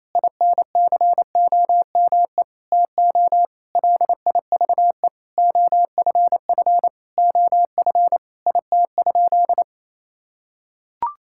Morse code, heard as a keyed tone, about 28 wpm, 700 Hz. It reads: INCOME TO LIVE OFF OF IT?
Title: Sentences from Top 1000 Words - Review Part 5 - 28wpm